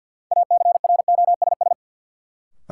Morse code, heard as a keyed tone, about 46 wpm, 700 Hz. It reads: ACROSS
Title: Sentences from Top 700 Words - New Word Drill - 50wpm